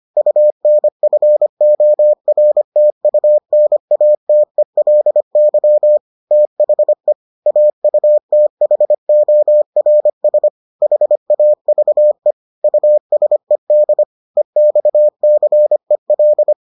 Morse code, heard as a keyed tone, about 25 wpm, 600 Hz. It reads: UNFORTUNATELY THE AUTHORS HAVE USED EXCEL